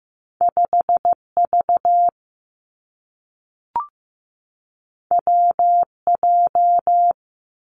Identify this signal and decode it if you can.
Morse code, heard as a keyed tone, about 15 wpm, 700 Hz. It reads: 5V WJ
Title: Two Challenging Letter-Number Combinations 15wpm